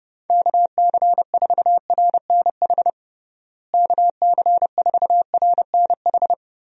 Morse code, heard as a keyed tone, about 32 wpm, 700 Hz. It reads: KC4RNH KC4RNH
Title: USA Call Signs - 6 Characters - Repeated 3x - 30wpm